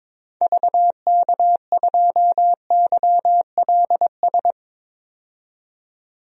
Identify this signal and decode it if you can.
Morse code, heard as a keyed tone, about 21 wpm, 700 Hz. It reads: VK2YLS
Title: DXCC-100 Call Signs - 6 Characters - 22wpm